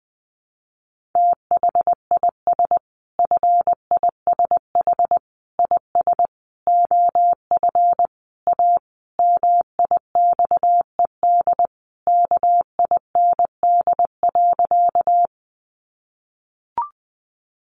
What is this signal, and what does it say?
Morse code, heard as a keyed tone, about 20 wpm, 700 Hz. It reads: THIS FISH IS OF A MIXED KIND.
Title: Sentences from Top 1500 Words 20wpm